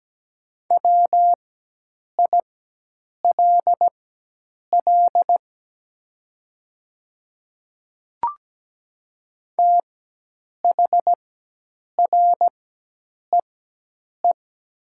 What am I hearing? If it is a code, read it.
Morse code, heard as a keyed tone, about 17 wpm, spaced as for 8 wpm, 690 Hz. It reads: WILL THREE
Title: 4x Character Spacing - Top 200 Words - 17wpm